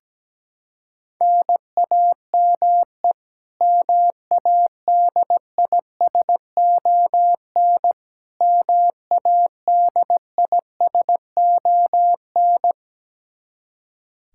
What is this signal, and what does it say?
Morse code, heard as a keyed tone, about 17 wpm, 700 Hz. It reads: NAME MADISON MADISON
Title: K1USN SST QSO Practice 17wpm